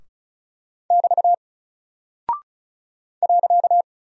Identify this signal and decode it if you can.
Morse code, heard as a keyed tone, about 36 wpm, 700 Hz. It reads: <BT> .
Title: Single Character (Letter:Number:Punctuation:Prosign) 35wpm